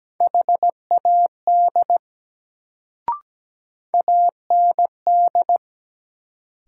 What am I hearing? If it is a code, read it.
Morse code, heard as a keyed tone, about 17 wpm, 700 Hz. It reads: HAD AND
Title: Sentences from Top 100 Words - New Word Drill - 17wpm